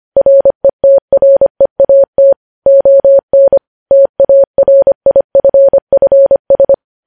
Morse code, heard as a keyed tone, about 25 wpm, 550 Hz. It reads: RETREAT ON TARIFFS
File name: podcast / News headlines-1x-25x25